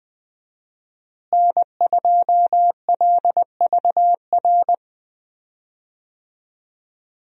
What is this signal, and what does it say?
Morse code, heard as a keyed tone, about 20 wpm, 700 Hz. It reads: N2LVR
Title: USA Call Signs - 5 Characters - Repeated 3x - 20wpm